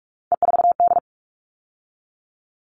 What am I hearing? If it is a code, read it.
Morse code, heard as a keyed tone, about 44 wpm, 700 Hz. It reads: E4D
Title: All Cut Numbers - 3-Digit 45wpm